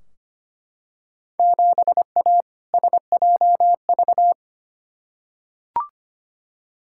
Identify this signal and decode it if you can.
Morse code, heard as a keyed tone, about 25 wpm, 700 Hz. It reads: 7A SJV
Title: Field Day Exchanges Encore - 25wpm